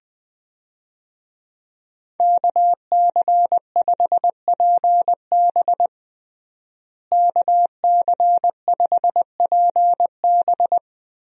Morse code, heard as a keyed tone, about 21 wpm, 700 Hz. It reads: KC5PB KC5PB
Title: USA Call Signs - 5 Characters - Repeated 3x - 20wpm